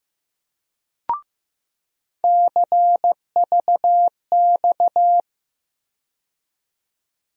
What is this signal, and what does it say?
Morse code, heard as a keyed tone, about 15 wpm, 700 Hz. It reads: CVX
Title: USA Call Sign Postfixes - 3 Letters 15wpm